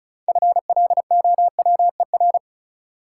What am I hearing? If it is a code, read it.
Morse code, heard as a keyed tone, about 34 wpm, 700 Hz. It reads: FLOWER
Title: Six Letter Words - 35wpm